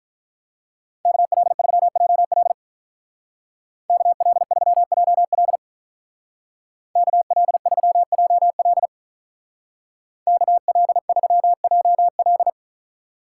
Morse code, heard as a keyed tone, about 47 wpm, 700 Hz. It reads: KL3JL KL3JL KL3JL KL3JL